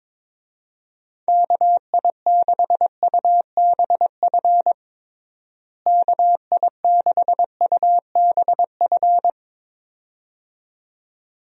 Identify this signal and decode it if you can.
Morse code, heard as a keyed tone, about 22 wpm, 700 Hz. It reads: KI6UBF KI6UBF
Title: USA Call Signs - 6 Characters - Repeated 2x - 22wpm